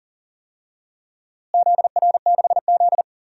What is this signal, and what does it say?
Morse code, heard as a keyed tone, about 40 wpm, 700 Hz. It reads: ZR6Z
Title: DXCC-100 Call Signs - 4 Characters - Repeated 3x - 40wpm